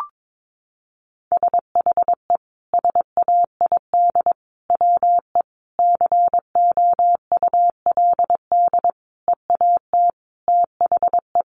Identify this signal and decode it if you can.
Morse code, heard as a keyed tone, about 22 wpm, 700 Hz. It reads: SHE SAID WE COULD EAT THE
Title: Sentences from Top 1300 Words 22wpm